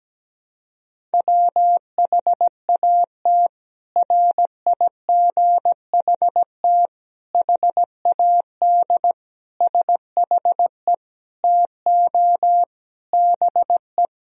Morse code, heard as a keyed tone, about 17 wpm, 700 Hz. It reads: WHAT RIGHT HAD SHE TO BE
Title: Sentences from Top 1000 Words - Review Part 1 - 17wpm